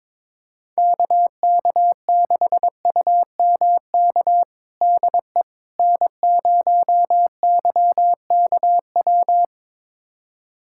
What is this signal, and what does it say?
Morse code, heard as a keyed tone, about 22 wpm, 700 Hz. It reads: KK6UMK DE N0YKW
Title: USA Callsign DE Callsign 22wpm